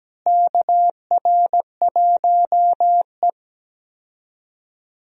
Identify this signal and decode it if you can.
Morse code, heard as a keyed tone, about 17 wpm, 700 Hz. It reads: KR1E
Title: USA Call Signs - 4 Characters - Repeated 2x - 17wpm